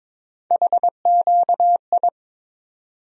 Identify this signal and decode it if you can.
Morse code, heard as a keyed tone, about 22 wpm, 700 Hz. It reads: HQI
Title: USA Call Sign Postfixes - 3 Letters 22wpm